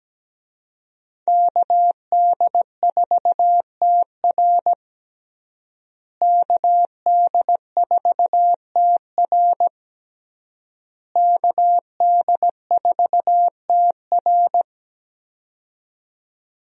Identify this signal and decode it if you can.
Morse code, heard as a keyed tone, about 17 wpm, 700 Hz. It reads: KD4TR KD4TR KD4TR
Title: USA Call Signs - 5 Characters - Repeated 3x - 17wpm